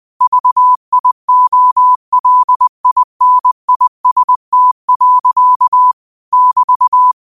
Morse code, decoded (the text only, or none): VIOLINIST. <BT>